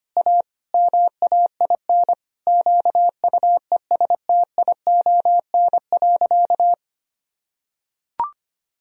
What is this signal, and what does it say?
Morse code, heard as a keyed tone, about 25 wpm, 700 Hz. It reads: A MAIN QUESTION.
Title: Sentences from Top 1000 Words - Review Part 2 - 25wpm